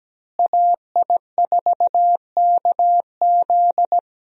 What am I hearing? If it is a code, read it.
Morse code, heard as a keyed tone, about 17 wpm, 700 Hz. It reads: AI4KZ
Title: USA Call Signs - 5 Characters - Repeated 3x - 17wpm